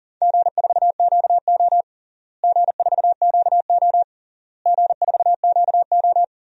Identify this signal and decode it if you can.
Morse code, heard as a keyed tone, about 37 wpm, 700 Hz. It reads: G4QO G4QO G4QO
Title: DXCC-100 Call Signs - 4 Characters - Repeated 3x - 40wpm